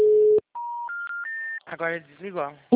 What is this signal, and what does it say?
Special information tone, with speech.